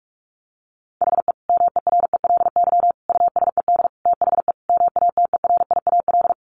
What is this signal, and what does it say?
Morse code, heard as a keyed tone, about 44 wpm, 700 Hz. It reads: HE MERELY USED THE MATERIAL
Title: Sentences from Top 1400 Words 45wpm